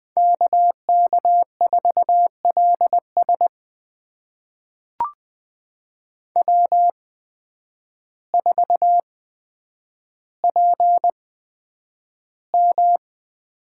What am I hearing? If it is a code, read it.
Morse code, heard as a keyed tone, about 20 wpm, spaced as for 9 wpm, 700 Hz. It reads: KK4LS W4PM
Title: USA Call Signs - 8x Character Spacing - 20wpm